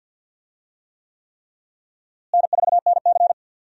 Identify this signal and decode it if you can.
Morse code, heard as a keyed tone, about 52 wpm, 700 Hz. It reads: N4NC
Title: USA Call Signs - 4 Characters - Repeated 3x - 50wpm